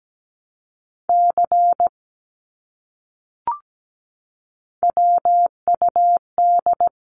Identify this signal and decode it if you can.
Morse code, heard as a keyed tone, about 17 wpm, 700 Hz. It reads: C WUD
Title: Abbreviations Repeated 3x - 17wpm